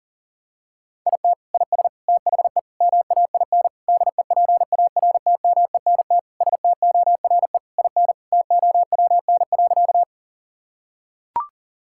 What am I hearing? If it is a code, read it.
Morse code, heard as a keyed tone, about 40 wpm, 700 Hz. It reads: IT IS THE MAIN DEPARTMENT STORE IN TOWN.
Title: Sentences from Top 1000 Words - Review Part 5 - 40wpm